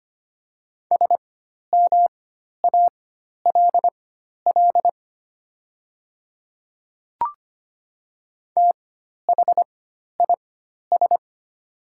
Morse code, heard as a keyed tone, about 25 wpm, spaced as for 12 wpm, 700 Hz. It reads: SMALL THIS
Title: 4x Character Spacing - Top 200 Words - 25wpm